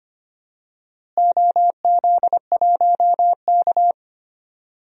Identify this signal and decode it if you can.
Morse code, heard as a keyed tone, about 25 wpm, 700 Hz. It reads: OZ1K
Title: DXCC-100 Call Signs - 4 Characters - Repeated 2x - 25wpm